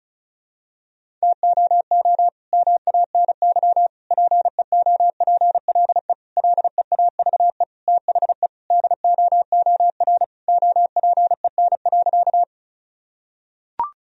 Morse code, heard as a keyed tone, about 34 wpm, 700 Hz. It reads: TOO MANY PEOPLE LEAVE THE DOOR OPEN.
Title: Sentences from Top 1000 Words - Review Part 1 - 35wpm